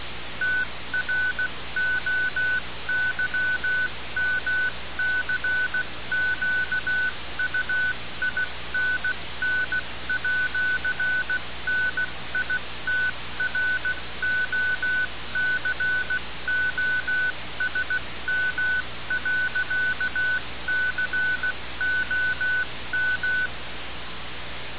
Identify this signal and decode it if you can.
Morse code, heard as a keyed tone, about 16 wpm, 1500 Hz, with noise. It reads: TROYMCQUINN@NITROCOSM.COM
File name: radiocap / REC z